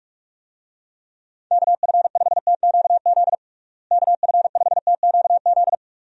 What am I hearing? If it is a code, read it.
Morse code, heard as a keyed tone, about 43 wpm, 700 Hz. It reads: KF5TQZ KF5TQZ
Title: USA Call Signs - 6 Characters - Repeated 2x - 45wpm